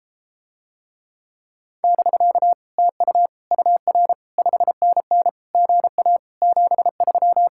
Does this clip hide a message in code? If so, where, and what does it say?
Yes: Morse code from 1.8 to 7.5 s, heard as a keyed tone, about 34 wpm, 700 Hz. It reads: <BK> TU UR 5NN GA 73